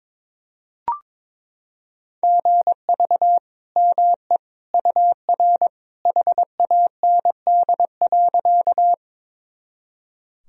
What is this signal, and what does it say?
Morse code, heard as a keyed tone, about 22 wpm, 700 Hz. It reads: GV ME UR HAND.
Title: Sentences from Top 200 Words and Abbreviations 22wpm